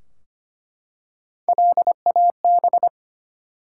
Morse code, heard as a keyed tone, about 25 wpm, 700 Hz. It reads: LAB
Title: Three Letter Words - 25wpm